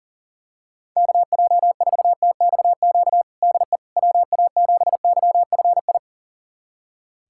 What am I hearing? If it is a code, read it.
Morse code, heard as a keyed tone, about 37 wpm, 700 Hz. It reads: KJ4TXQ DE WA7YFI
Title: USA Callsign DE Callsign - 6 Characters - 40wpm